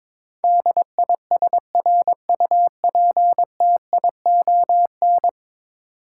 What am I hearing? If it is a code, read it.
Morse code, heard as a keyed tone, about 22 wpm, 700 Hz. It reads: DISRUPTION
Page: morse-code-ninja-practice-22wpm